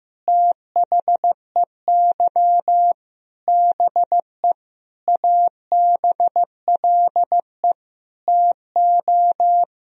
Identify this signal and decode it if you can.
Morse code, heard as a keyed tone, about 15 wpm, 700 Hz. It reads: THEY BE ABLE TO
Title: Sentences from Top 1000 Words - Review Part 2 - 15wpm